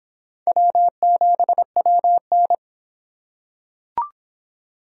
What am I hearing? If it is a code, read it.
Morse code, heard as a keyed tone, about 26 wpm, 700 Hz. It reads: W7WN